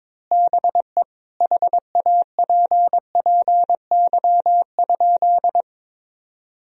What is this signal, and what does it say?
Morse code, heard as a keyed tone, about 22 wpm, 700 Hz. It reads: BE HAPPY?
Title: Sentences from Top 1000 Words - Review Part 2 - 22wpm